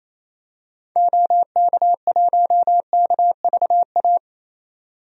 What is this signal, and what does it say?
Morse code, heard as a keyed tone, about 28 wpm, 700 Hz. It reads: OK1KVA
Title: DXCC-100 Call Signs - 6 Characters - 28wpm